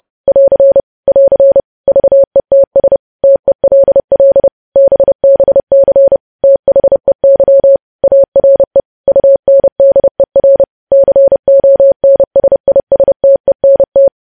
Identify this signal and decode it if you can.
Morse code, heard as a keyed tone, about 30 wpm, 550 Hz. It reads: <AR> <AR> VETS TELL BBC THEY ARE UNDER CONSISTENT
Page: news headlines in morse code at 30 wpm